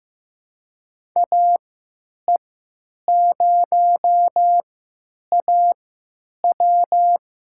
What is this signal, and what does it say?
Morse code, heard as a keyed tone, about 15 wpm, spaced as for 9 wpm, 700 Hz. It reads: AE0AW
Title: USA Call Signs - 5 Characters - 3x Character Spacing - 15wpm